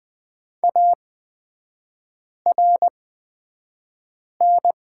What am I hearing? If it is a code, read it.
Morse code, heard as a keyed tone, about 20 wpm, spaced as for 5 wpm, 700 Hz. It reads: ARN
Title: 255 Lesson 41e - Period - Three Characters 20wpm